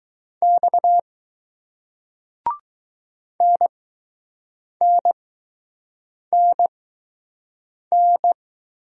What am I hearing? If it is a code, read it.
Morse code, heard as a keyed tone, about 20 wpm, spaced as for 6 wpm, 700 Hz. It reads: X NNNN